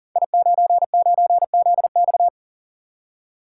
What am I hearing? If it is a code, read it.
Morse code, heard as a keyed tone, about 39 wpm, 700 Hz. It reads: I99ZX
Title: Code Groups - 5 Letter:Numbers - 40wpm